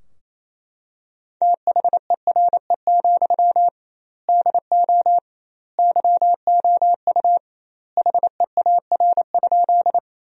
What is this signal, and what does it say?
Morse code, heard as a keyed tone, about 28 wpm, 700 Hz. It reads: THERE, DO YOU HEAR?
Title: 2x Word-Spacing - Sentences from Top 300 Words - 28wpm